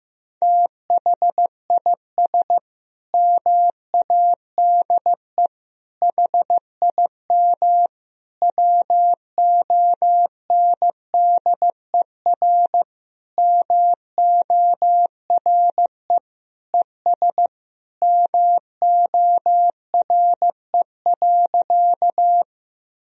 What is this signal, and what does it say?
Morse code, heard as a keyed tone, about 15 wpm, 700 Hz. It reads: THIS MADE HIM WONDER MORE ES MORE.
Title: Sentences from Top 500 Words and Abbreviations 15wpm